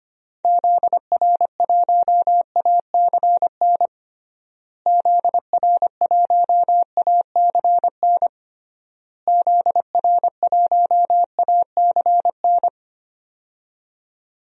Morse code, heard as a keyed tone, about 25 wpm, 700 Hz. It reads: ZR1ACN ZR1ACN ZR1ACN
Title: DXCC-100 Call Signs - 6 Characters - Repeated 3x - 25wpm